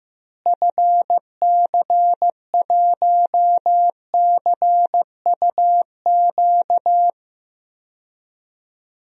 Morse code, heard as a keyed tone, about 15 wpm, 700 Hz. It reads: FC1CUQ
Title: DXCC-100 Call Signs - 6 Characters - Repeated 3x - 15wpm